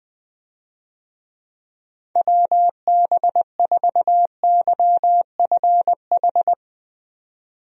Morse code, heard as a keyed tone, about 20 wpm, 700 Hz. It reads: WB4YFH
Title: USA Call Signs - 6 Characters - 20wpm